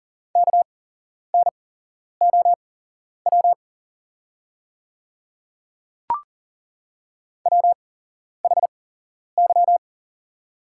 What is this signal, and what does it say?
Morse code, heard as a keyed tone, about 38 wpm, spaced as for 11 wpm, 700 Hz. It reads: KNOW WHY
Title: 8x Character Spacing - Top 200 Words - 40wpm